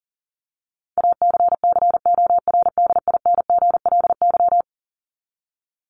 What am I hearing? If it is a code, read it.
Morse code, heard as a keyed tone, about 40 wpm, 700 Hz. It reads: ACCORDINGLY
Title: LY Suffix - Top 100 Words with Base Words 40wpm